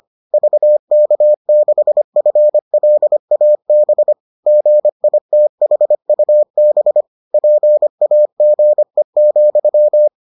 Morse code, heard as a keyed tone, about 25 wpm, 600 Hz. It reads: VK6FLAB GITHUB PAGE,